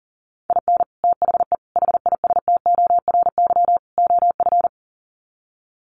Morse code, heard as a keyed tone, about 40 wpm, 700 Hz. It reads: IN THE HISTORY OF
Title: Sets of 4 Words 40wpm